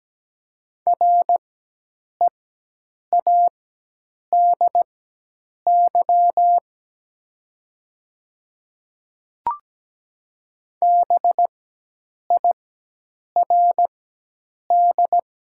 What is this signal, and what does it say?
Morse code, heard as a keyed tone, about 17 wpm, spaced as for 8 wpm, 700 Hz. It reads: READY BIRD
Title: 4x Character Spacing - Top 500 Words - 17wpm